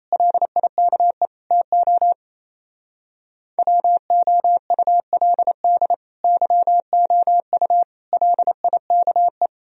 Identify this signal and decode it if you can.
Morse code, heard as a keyed tone, about 36 wpm, 700 Hz. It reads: LIKE TO WOULD YOU LIKE